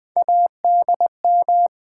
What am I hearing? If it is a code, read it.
Morse code, heard as a keyed tone, about 20 wpm, 700 Hz. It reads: ADM
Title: Sentences from Top 1300 Words - New Word Drill - 20wpm